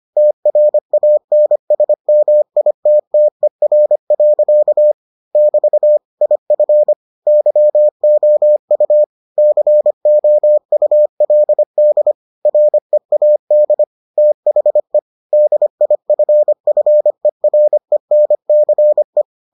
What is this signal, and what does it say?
Morse code, heard as a keyed tone, about 25 wpm, 600 Hz. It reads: TRANSMITTER. <BT> IF YOU COULD READ THE DIFFERENCE